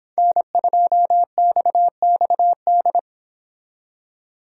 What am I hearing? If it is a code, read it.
Morse code, heard as a keyed tone, about 28 wpm, 700 Hz. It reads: N2XXD